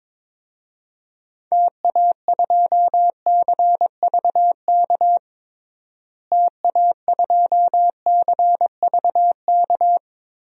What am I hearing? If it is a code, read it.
Morse code, heard as a keyed tone, about 22 wpm, 700 Hz. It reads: TA2CVK TA2CVK
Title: DXCC-100 Call Signs - 6 Characters - Repeated 3x - 22wpm